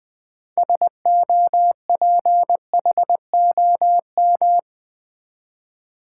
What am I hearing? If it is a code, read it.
Morse code, heard as a keyed tone, about 20 wpm, 700 Hz. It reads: SOPHOM
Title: Build A Word - 3 Letters At A Time - 9-Letter Words - 20wpm